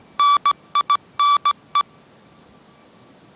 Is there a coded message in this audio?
Yes: Morse code from 0.2 to 1.8 s, heard as a keyed tone, about 17 wpm, 1160 Hz. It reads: NINE